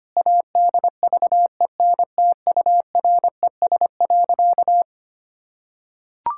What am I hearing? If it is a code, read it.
Morse code, heard as a keyed tone, about 25 wpm, 700 Hz. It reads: ADVENTURES.
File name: practice / 25wpm / Sentences with 2-Word Phrases - 25wpm